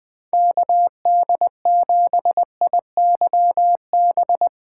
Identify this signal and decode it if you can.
Morse code, heard as a keyed tone, about 20 wpm, 700 Hz. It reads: KD7IYB
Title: USA Call Signs - 6 Characters - 6x Character Spacing - 20wpm